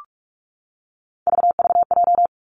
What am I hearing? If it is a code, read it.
Morse code, heard as a keyed tone, about 46 wpm, 700 Hz. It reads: VVJ
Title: Three Challenging Letter-Number Combinations 45wpm